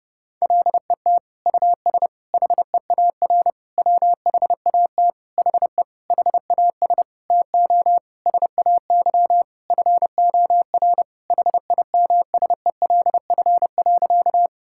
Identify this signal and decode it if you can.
Morse code, heard as a keyed tone, about 30 wpm, 700 Hz. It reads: LET US HEAR WHAT HE HAS TO SAY FOR HIMSELF.
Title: Sentences from Top 1600 Words 30wpm